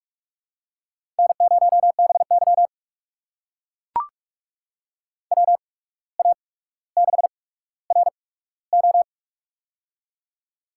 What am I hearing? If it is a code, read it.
Morse code, heard as a keyed tone, about 44 wpm, spaced as for 18 wpm, 700 Hz. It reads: N0BY WA6RO
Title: USA Call Signs - 8x Character Spacing - 45wpm